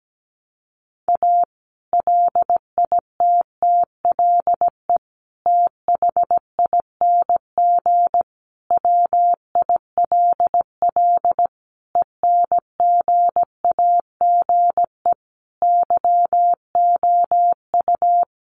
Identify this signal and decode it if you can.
Morse code, heard as a keyed tone, about 17 wpm, 700 Hz. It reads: A LITTLE THING WILL ENGAGE YOU
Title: Sentences from Top 1500 Words 17wpm